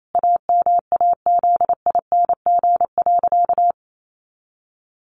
Morse code, heard as a keyed tone, about 28 wpm, 700 Hz. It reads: AMAZING.
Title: Sentences from Top 1500 Words 28wpm